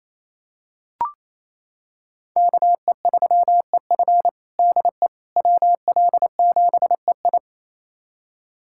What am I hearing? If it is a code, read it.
Morse code, heard as a keyed tone, about 28 wpm, 700 Hz. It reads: KE3EF DE WL7EI
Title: USA Callsign DE Callsign - 5 Characters - 28wpm